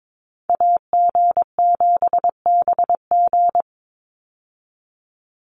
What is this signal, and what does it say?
Morse code, heard as a keyed tone, about 22 wpm, 700 Hz. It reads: AG7BG
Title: USA Call Signs 22wpm